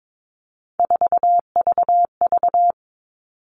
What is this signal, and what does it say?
Morse code, heard as a keyed tone, about 22 wpm, 700 Hz. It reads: 4VV
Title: All Cut Numbers - 3-Digit 22wpm